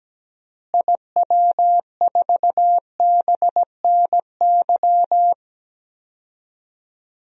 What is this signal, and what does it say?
Morse code, heard as a keyed tone, about 17 wpm, 700 Hz. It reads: IW4BNY
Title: DXCC-100 Call Signs - 6 Characters - Repeated 3x - 17wpm